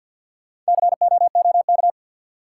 Morse code, heard as a keyed tone, about 52 wpm, 700 Hz. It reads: COOK
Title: Sentences from Top 700 Words - New Word Drill - 50wpm